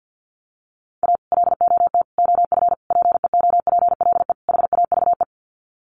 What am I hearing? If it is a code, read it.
Morse code, heard as a keyed tone, about 50 wpm, 700 Hz. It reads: A LOT OF PEOPLE HAVE
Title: Sets of 5 Words 50wpm